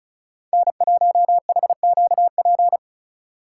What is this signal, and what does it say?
Morse code, heard as a keyed tone, about 43 wpm, 700 Hz. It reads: N1HQP
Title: Speed Racing - USA 1x3 Call Signs 35wpm